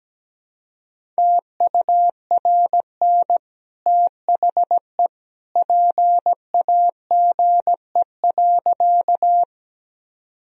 Morse code, heard as a keyed tone, about 17 wpm, 700 Hz. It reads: TURN THE PAGE.
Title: Sentences from Top 1000 Words - Review Part 1 - 17wpm